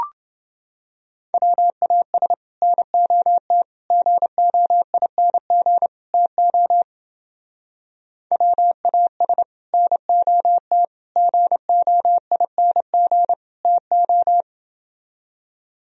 Morse code, heard as a keyed tone, about 29 wpm, 700 Hz. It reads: WAS NOT GOING TO WAS NOT GOING TO